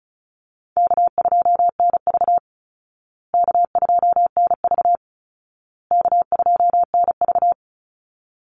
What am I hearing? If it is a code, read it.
Morse code, heard as a keyed tone, about 34 wpm, 700 Hz. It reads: K2NV K2NV K2NV
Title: Call Signs Repeated 2x - 35wpm